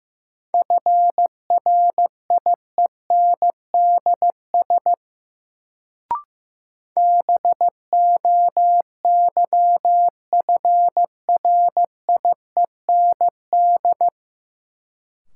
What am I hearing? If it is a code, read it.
Morse code, heard as a keyed tone, about 15 wpm, 700 Hz. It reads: FRIENDS BOYFRIEND
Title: Sentences from Top 1000 Words - Review Part 5 - New Word Drill - 15wpm